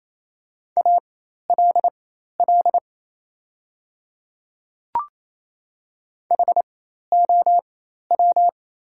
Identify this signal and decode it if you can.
Morse code, heard as a keyed tone, about 27 wpm, spaced as for 13 wpm, 700 Hz. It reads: ALL HOW
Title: 4x Character Spacing - Top 100 Words - 28wpm